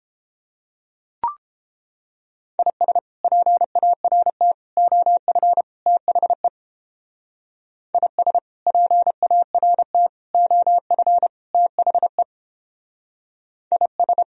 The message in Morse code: IS PART OF THE IS PART OF THE IS

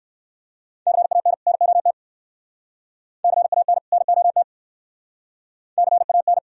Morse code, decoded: CAN NOT CAN NOT CAN

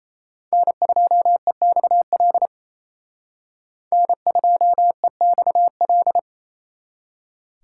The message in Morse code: N2EXL N2EXL

